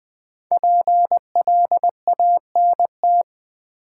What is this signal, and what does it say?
Morse code, heard as a keyed tone, about 21 wpm, 700 Hz. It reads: PLANT
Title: Sentences from Top 1000 Words - Review Part 1 - New Word Drill - 20wpm